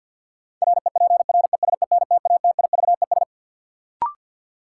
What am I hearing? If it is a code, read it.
Morse code, heard as a keyed tone, about 50 wpm, 700 Hz. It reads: REPRESENTATIVES
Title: Sets of 5 Words Encore - New Word Drill - 50wpm